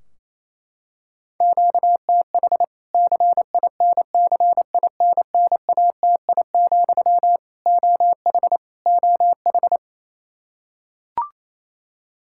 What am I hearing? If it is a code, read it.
Morse code, heard as a keyed tone, about 28 wpm, 700 Hz. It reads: QTH CINCINNATI, OH OH
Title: K1USN SST QSO Practice 28wpm